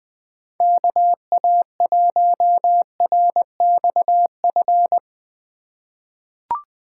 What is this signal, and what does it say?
Morse code, heard as a keyed tone, about 21 wpm, 700 Hz. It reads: KA1RXF
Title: USA Call Signs - 6 Characters - 3x Character Spacing - 20wpm